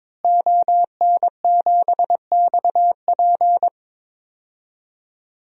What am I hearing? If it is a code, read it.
Morse code, heard as a keyed tone, about 22 wpm, 700 Hz. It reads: ON7XP
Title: DXCC-100 Call Signs - 5 Characters - Repeated 2x - 22wpm